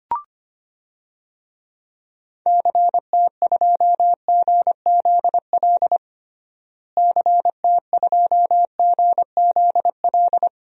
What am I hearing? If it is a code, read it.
Morse code, heard as a keyed tone, about 25 wpm, 700 Hz. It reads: CT2GZL CT2GZL
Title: DXCC-100 Call Signs - 6 Characters - Repeated 2x - 25wpm